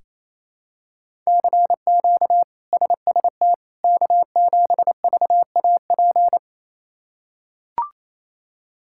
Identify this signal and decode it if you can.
Morse code, heard as a keyed tone, about 28 wpm, 700 Hz. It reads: CQ SST K7VAP
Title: K1USN SST QSO Practice - 28wpm